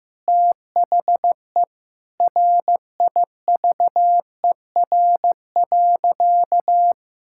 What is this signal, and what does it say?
Morse code, heard as a keyed tone, about 15 wpm, 700 Hz. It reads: THE RIVER.
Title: Sentences from Top 1600 Words 15wpm